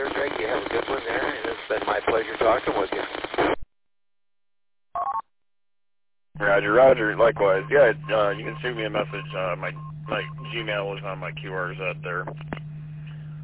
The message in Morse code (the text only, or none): none